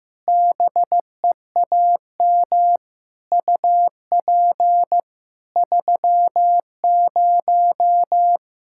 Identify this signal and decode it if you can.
Morse code, heard as a keyed tone, about 15 wpm, 700 Hz. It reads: BEAM UP 30
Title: Practice QSO - Exchange 2 - Info - 15wpm